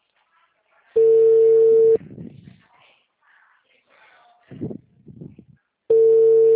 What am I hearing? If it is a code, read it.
Ringback tone (China, 450 Hz).